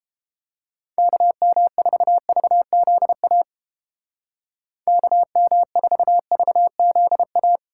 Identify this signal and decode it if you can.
Morse code, heard as a keyed tone, about 32 wpm, 700 Hz. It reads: KM4VZA KM4VZA